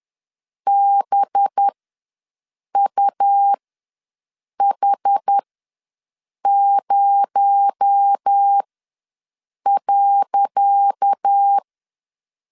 Morse code, heard as a keyed tone, about 10 wpm, spaced as for 6 wpm, 800 Hz. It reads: BUH0.
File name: LEZIONI / 10 SIGNS / 10D COMA ws 10wpm 4314